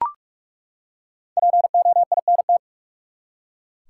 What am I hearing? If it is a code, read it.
Morse code, heard as a keyed tone, about 42 wpm, 700 Hz. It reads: POINT
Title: Five Letter Words - 45wpm